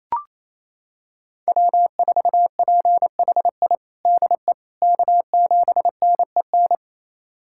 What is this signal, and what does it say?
Morse code, heard as a keyed tone, about 28 wpm, 700 Hz. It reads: W4PHI DE K7NEN
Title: USA Callsign DE Callsign - 5 Characters - 28wpm